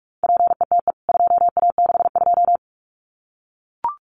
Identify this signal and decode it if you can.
Morse code, heard as a keyed tone, about 44 wpm, 700 Hz. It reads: PETE 2A62
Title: CWT Exchange 45wpm